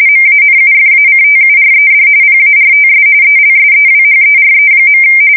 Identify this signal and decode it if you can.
RTTY, about 45.45 baud. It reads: MANUFACTURED PARTS ELSEWHERE.
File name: fsk170 / battleshipnj